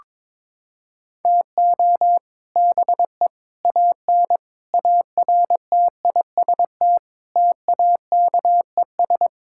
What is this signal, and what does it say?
Morse code, heard as a keyed tone, about 22 wpm, 700 Hz. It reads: TO BE AN ARTIST TAKES
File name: practice / 22wpm / Sentences with 3-Word Phrases - 22wpm